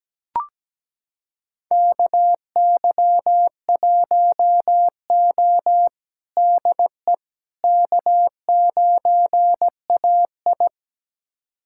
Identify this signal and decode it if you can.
Morse code, heard as a keyed tone, about 17 wpm, 700 Hz. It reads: KY1O DE K9AI
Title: USA Callsign DE Callsign - 4 Characters - 17wpm